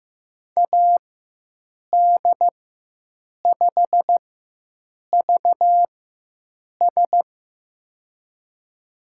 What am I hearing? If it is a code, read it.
Morse code, heard as a keyed tone, about 15 wpm, spaced as for 7 wpm, 700 Hz. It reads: AD5VS
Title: USA Call Signs - 4x Character Spacing - 15wpm